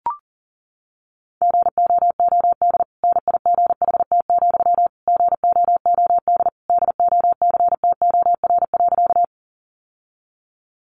Morse code, heard as a keyed tone, about 40 wpm, 700 Hz. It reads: GOOD NIGHT, GOOD DOCTOR.
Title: Sentences from Top 1500 Words 40wpm